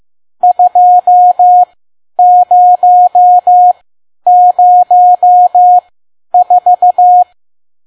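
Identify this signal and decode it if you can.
Morse code, heard as a keyed tone, about 15 wpm, spaced as for 10 wpm, 720 Hz. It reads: 2004
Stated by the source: Morse code practice at 10 WPM.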